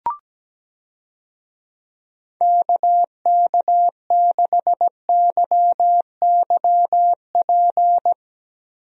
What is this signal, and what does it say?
Morse code, heard as a keyed tone, about 17 wpm, 700 Hz. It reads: KK6YYP
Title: USA Call Signs - 6 Characters - Repeated 2x - 17wpm